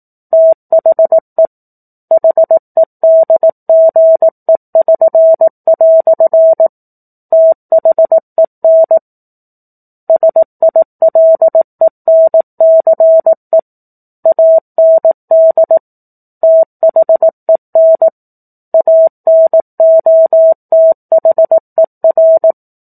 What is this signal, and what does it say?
Morse code, heard as a keyed tone, about 18 wpm, 650 Hz. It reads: THE HEDGE<SN>" THEN SILENCE AND THEN ANOTHER
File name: 04-Chapter-04 18 WPM